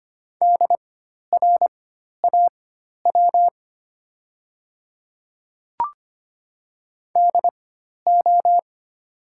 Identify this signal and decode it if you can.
Morse code, heard as a keyed tone, about 25 wpm, spaced as for 12 wpm, 700 Hz. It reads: DRAW DO
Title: 4x Character Spacing - Top 300 Words - 25wpm